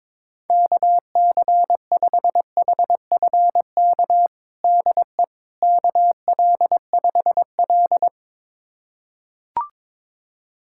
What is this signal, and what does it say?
Morse code, heard as a keyed tone, about 22 wpm, 700 Hz. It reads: KC5HFK DE KL5L
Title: USA Callsign DE Callsign 22wpm